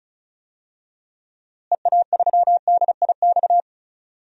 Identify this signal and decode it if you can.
Morse code, heard as a keyed tone, about 34 wpm, 700 Hz. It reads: EA3DIX
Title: DXCC-100 Call Signs - 6 Characters - Repeated 3x - 35wpm